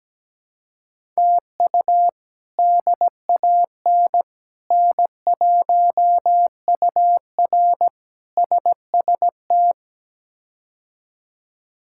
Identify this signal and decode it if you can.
Morse code, heard as a keyed tone, about 17 wpm, 700 Hz. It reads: TU DAN N1UR SST
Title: K1USN SST QSO Practice - 17wpm